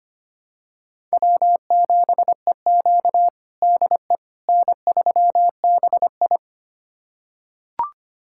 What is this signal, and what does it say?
Morse code, heard as a keyed tone, about 25 wpm, 700 Hz. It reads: W7EQ DE N3BI
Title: USA Callsign DE Callsign - 4 Characters - 25wpm